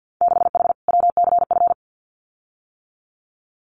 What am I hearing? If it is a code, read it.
Morse code, heard as a keyed tone, about 50 wpm, 700 Hz. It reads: 6H WCF
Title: Winter Field Day Exchanges Encore 50wpm